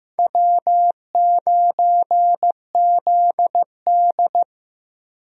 Morse code, heard as a keyed tone, about 15 wpm, 700 Hz. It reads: W9ZD
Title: USA Call Signs - 4 Characters - Repeated 2x - 15wpm